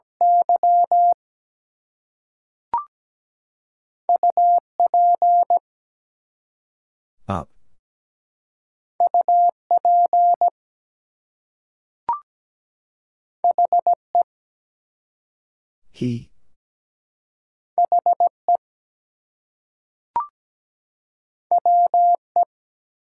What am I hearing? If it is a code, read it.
Morse code, heard as a keyed tone, about 17 wpm, 700 Hz. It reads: Y UP UP HE HE WE
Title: Two Letter Words - 17wpm